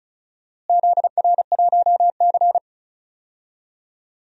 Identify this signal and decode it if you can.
Morse code, heard as a keyed tone, about 34 wpm, 700 Hz. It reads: ZR1C
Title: DXCC-100 Call Signs - 4 Characters - Repeated 3x - 35wpm